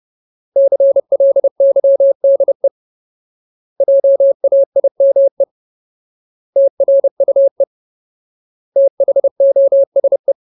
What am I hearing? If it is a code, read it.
Morse code, heard as a keyed tone, about 30 wpm, 550 Hz. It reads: CLYDE JAIME TRUE THOSE
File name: Ultimate Word List / 30WPM Ultimate